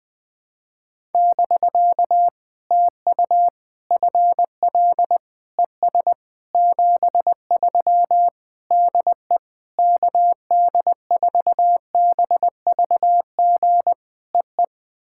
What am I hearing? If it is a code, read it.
Morse code, heard as a keyed tone, about 20 wpm, 700 Hz. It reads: <BK> TU FL ES 73 DE KD4BVG EE
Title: POTA QSO Practice - 20wpm